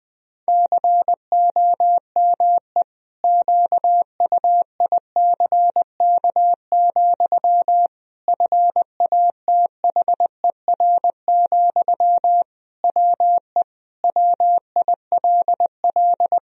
Morse code, heard as a keyed tone, about 20 wpm, 700 Hz. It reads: COME QUICK, FATHER, WE WILL
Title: Sentences from Top 1000 Words - Review Part 3 - 20wpm